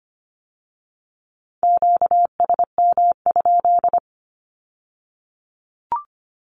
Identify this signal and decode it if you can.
Morse code, heard as a keyed tone, about 25 wpm, 700 Hz. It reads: QSM?
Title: Q-Codes with Spoken Meaning - 25wpm